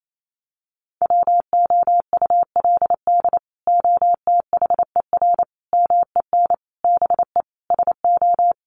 Morse code, heard as a keyed tone, about 28 wpm, 700 Hz. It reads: WOULD OTHER MEN BE SO
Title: Sentences from Top 1400 Words 28wpm